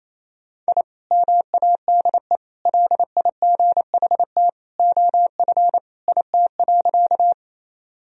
Morse code, heard as a keyed tone, about 28 wpm, 700 Hz. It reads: I MADE LIGHT OF IT.
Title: Sentences from Top 1000 Words - Review Part 1 - 28wpm